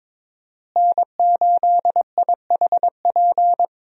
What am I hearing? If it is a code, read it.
Morse code, heard as a keyed tone, about 22 wpm, 700 Hz. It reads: N8IHP
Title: USA Call Signs - 5 Characters - Repeated 3x - 22wpm